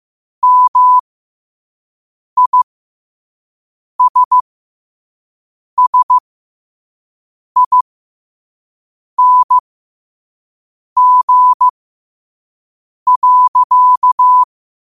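Morse code, heard as a keyed tone, about 15 wpm, spaced as for 5 wpm, 1000 Hz. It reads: MISSING.
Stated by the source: Morse code practice at 5 wpm.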